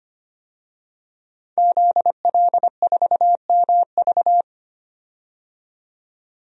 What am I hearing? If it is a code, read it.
Morse code, heard as a keyed tone, about 25 wpm, 700 Hz. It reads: ZL4MV
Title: DXCC-100 Call Signs - 5 Characters - 25wpm